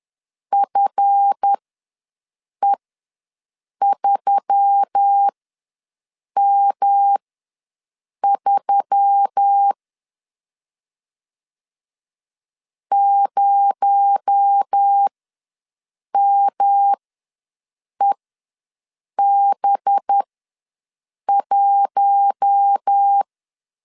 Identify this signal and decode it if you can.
Morse code, heard as a keyed tone, about 10 wpm, spaced as for 6 wpm, 800 Hz. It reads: FE3M3 0MEB1
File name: LEZIONI / 7 GF9Z / 7 GF ws 10wpm 3918